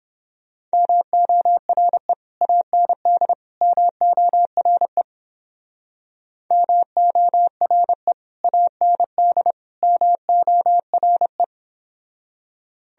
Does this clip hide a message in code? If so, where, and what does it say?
Yes: Morse code from 0.7 to 11.4 s, heard as a keyed tone, about 29 wpm, 700 Hz. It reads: MORE AND MORE MORE AND MORE